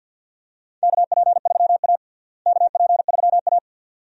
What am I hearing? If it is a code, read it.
Morse code, heard as a keyed tone, about 51 wpm, 700 Hz. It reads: KP3A KP3A
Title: USA Call Signs - 4 Characters - Repeated 2x - 50wpm